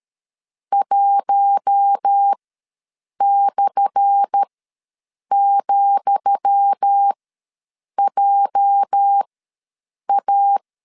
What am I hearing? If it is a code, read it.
Morse code, heard as a keyed tone, about 13 wpm, spaced as for 7 wpm, 800 Hz. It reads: 1/,JA
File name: LEZIONI / 10 SIGNS / 10D COMA ws 12wpm 3618